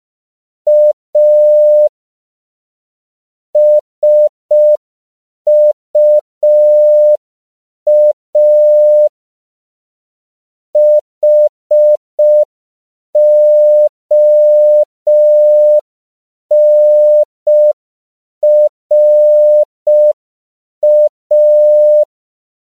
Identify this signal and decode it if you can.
Morse code, heard as a keyed tone, about 5 wpm, 600 Hz. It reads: A SUA HONRA